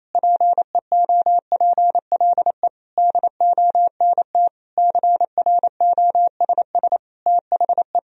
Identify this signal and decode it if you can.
Morse code, heard as a keyed tone, about 28 wpm, 700 Hz. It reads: PEOPLE DONT CROSS THE
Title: Sentences from Top 1000 Words - Review Part 3 - 28wpm